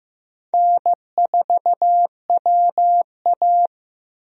Morse code, heard as a keyed tone, about 15 wpm, 700 Hz. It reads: N4WA
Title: USA Call Signs - 4 Characters - Repeated 3x - 15wpm